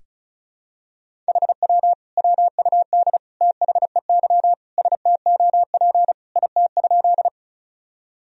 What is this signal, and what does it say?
Morse code, heard as a keyed tone, about 36 wpm, 700 Hz. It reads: HW WUD THEY STOP IT?
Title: Sentences from Top 300 Words and Abbreviations 35wpm